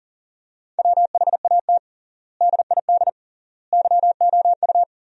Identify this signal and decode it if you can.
Morse code, heard as a keyed tone, about 40 wpm, 700 Hz. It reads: WHAT DID YOU
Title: 3x Word-Spacing - Sentences from Top 200 Words - 40wpm